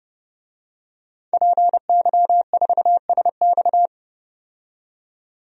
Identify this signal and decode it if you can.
Morse code, heard as a keyed tone, about 32 wpm, 700 Hz. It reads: PY4SX
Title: DXCC-100 Call Signs - 5 Characters - 30wpm